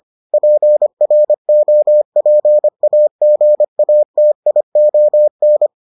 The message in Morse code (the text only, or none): PROPAGATION